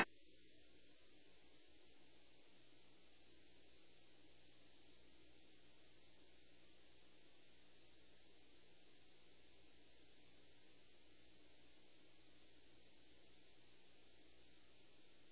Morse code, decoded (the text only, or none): none